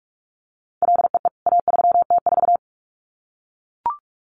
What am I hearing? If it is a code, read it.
Morse code, heard as a keyed tone, about 44 wpm, 700 Hz. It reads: LEE A3T4
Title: CWT Exchange 45wpm